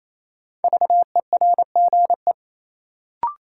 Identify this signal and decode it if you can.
Morse code, heard as a keyed tone, about 28 wpm, 700 Hz. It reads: VERGE
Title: Sets of 4 Words - New Word Drill - 28wpm